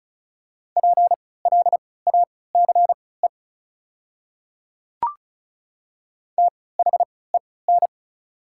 Morse code, heard as a keyed tone, about 34 wpm, spaced as for 20 wpm, 690 Hz. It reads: PLACE THEN
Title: 3x Character Spacing - Top 300 Words - 35wpm